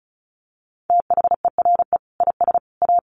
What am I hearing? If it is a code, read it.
Morse code, heard as a keyed tone, about 34 wpm, 700 Hz. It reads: THERE IS A
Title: Sentences from Top 1500 Words 35wpm